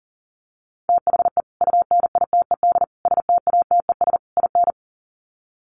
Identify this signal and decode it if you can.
Morse code, heard as a keyed tone, about 40 wpm, 700 Hz. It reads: THE UNITED STATES IN
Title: Sets of 4 Words 40wpm